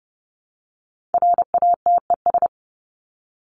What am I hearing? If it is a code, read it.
Morse code, heard as a keyed tone, about 30 wpm, 700 Hz. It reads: RATES
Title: ES:S Suffix - Top 100 Words with Base Words 30wpm